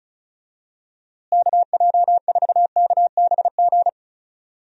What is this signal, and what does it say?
Morse code, heard as a keyed tone, about 36 wpm, 700 Hz. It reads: KJ4KBG
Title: USA Call Signs - 6 Characters - Repeated 2x - 35wpm